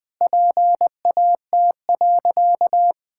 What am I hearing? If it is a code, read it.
Morse code, heard as a keyed tone, about 20 wpm, 700 Hz. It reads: PAT.
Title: Pangram Challenge v2 - 20wpm